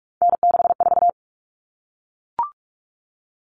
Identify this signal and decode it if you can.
Morse code, heard as a keyed tone, about 44 wpm, 700 Hz. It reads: N64
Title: Common Cut Numbers - 3-Digit 45wpm